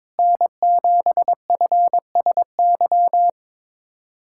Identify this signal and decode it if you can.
Morse code, heard as a keyed tone, about 21 wpm, 700 Hz. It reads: N7FSY
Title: USA Callsign DE Callsign - 5 Characters - 22wpm